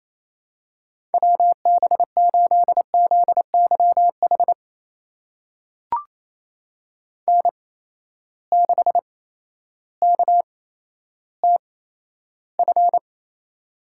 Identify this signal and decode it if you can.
Morse code, heard as a keyed tone, about 27 wpm, spaced as for 13 wpm, 700 Hz. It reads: WB8ZYH N6KTF
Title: USA Call Signs - 8x Character Spacing - 28wpm